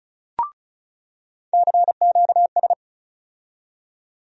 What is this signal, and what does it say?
Morse code, heard as a keyed tone, about 34 wpm, 700 Hz. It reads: CQS
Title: USA Call Sign Postfixes - 3 Letters 35wpm